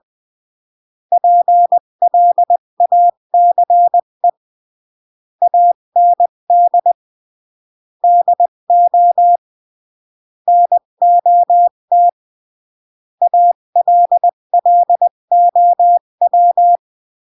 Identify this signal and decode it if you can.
Morse code, heard as a keyed tone, about 20 wpm, 700 Hz. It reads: PLACE AND DO NOT ALLOW
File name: CW / jodorowsky / 82 commandments 20-15